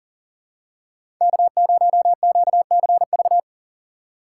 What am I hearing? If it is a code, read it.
Morse code, heard as a keyed tone, about 40 wpm, 700 Hz. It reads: K0QCV
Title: USA Call Signs - 5 Characters - 2x Character Spacing - 40wpm